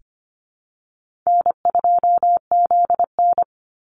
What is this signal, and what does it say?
Morse code, heard as a keyed tone, about 25 wpm, 700 Hz. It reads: N2ZN
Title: Easy Call Signs - 25wpm